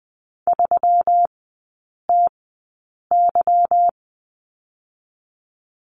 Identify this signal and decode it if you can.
Morse code, heard as a keyed tone, about 20 wpm, spaced as for 8 wpm, 690 Hz. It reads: 3TY
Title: 145 Lesson 24e - Letter y - Three Characters 20wpm